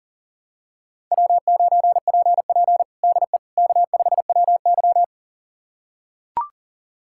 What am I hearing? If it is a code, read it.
Morse code, heard as a keyed tone, about 40 wpm, 700 Hz. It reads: W9PP DE K5WY
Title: USA Callsign DE Callsign - 4 Characters - 40wpm